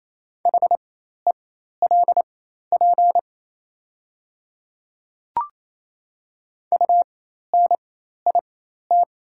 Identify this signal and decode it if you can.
Morse code, heard as a keyed tone, about 28 wpm, spaced as for 13 wpm, 690 Hz. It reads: HELP UNIT
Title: 4x Character Spacing - Top 500 Words - 28wpm